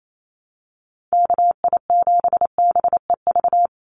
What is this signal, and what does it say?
Morse code, heard as a keyed tone, about 28 wpm, 700 Hz. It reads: KI7BEV
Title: USA Call Signs 28wpm